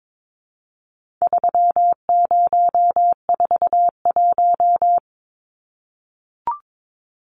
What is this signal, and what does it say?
Morse code, heard as a keyed tone, about 22 wpm, 700 Hz. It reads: 3041
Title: Numbers - 4-Digit 22wpm